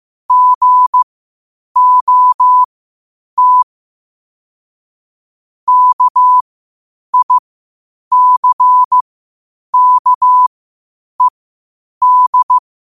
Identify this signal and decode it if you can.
Morse code, heard as a keyed tone, about 15 wpm, spaced as for 8 wpm, 1000 Hz. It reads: GOT KICKED
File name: qotdfiles / 8wpm